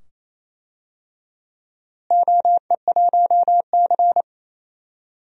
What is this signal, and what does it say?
Morse code, heard as a keyed tone, about 28 wpm, 700 Hz. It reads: OE1C
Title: DXCC-100 Call Signs - 4 Characters - Repeated 2x - 28wpm